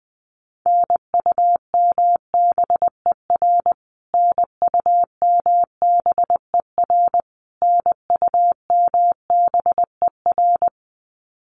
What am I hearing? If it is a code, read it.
Morse code, heard as a keyed tone, about 20 wpm, 700 Hz. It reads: NUMBER NUMBER NUMBER
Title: Top 500 Words Repeated 20wpm